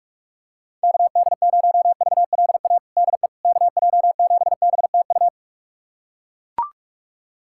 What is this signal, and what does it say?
Morse code, heard as a keyed tone, about 41 wpm, 690 Hz. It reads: KD0VLA DE KJ7BTU
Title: USA Callsign DE Callsign - 6 Characters - 45wpm